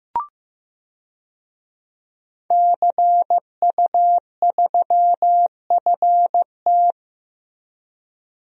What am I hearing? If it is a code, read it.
Morse code, heard as a keyed tone, about 15 wpm, 700 Hz. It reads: CU3FT
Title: DXCC-100 Call Signs - 5 Characters - Repeated 3x - 15wpm